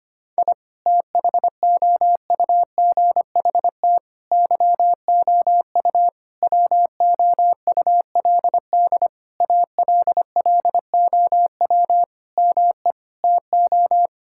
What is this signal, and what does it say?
Morse code, heard as a keyed tone, about 25 wpm, 700 Hz. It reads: I THOUGHT YOU WOULD ALLOW ME TO
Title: Sentences from Top 1000 Words - Review Part 5 - 25wpm